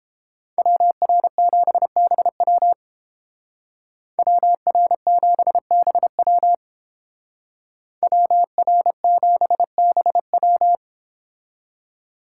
Morse code, heard as a keyed tone, about 30 wpm, 700 Hz. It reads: WR7BW WR7BW WR7BW